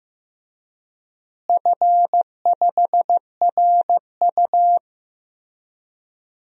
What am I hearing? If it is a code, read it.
Morse code, heard as a keyed tone, about 15 wpm, 700 Hz. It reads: F5RU
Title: DXCC-100 Call Signs - 4 Characters - Repeated 3x - 15wpm